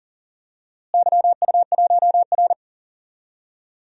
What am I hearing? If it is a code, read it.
Morse code, heard as a keyed tone, about 37 wpm, 700 Hz. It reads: YU1R
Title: DXCC-100 Call Signs - 4 Characters - Repeated 2x - 40wpm